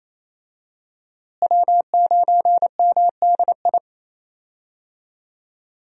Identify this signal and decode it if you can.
Morse code, heard as a keyed tone, about 28 wpm, 700 Hz. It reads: W9MDI
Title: USA Call Signs - 5 Characters - Repeated 3x - 28wpm